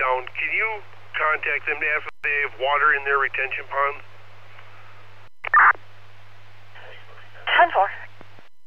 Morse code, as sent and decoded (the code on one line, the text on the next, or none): none